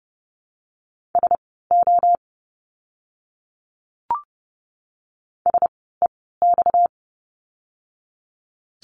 Morse code, heard as a keyed tone, about 30 wpm, spaced as for 17 wpm, 700 Hz. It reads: SO SEX
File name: course / 30:10wpm / 230 Lesson 37f - Letter x - Words 30wpm